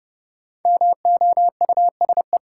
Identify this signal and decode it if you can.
Morse code, heard as a keyed tone, about 30 wpm, 700 Hz. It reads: MOUSE
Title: Sentences from Top 700 Words - New Word Drill - 30wpm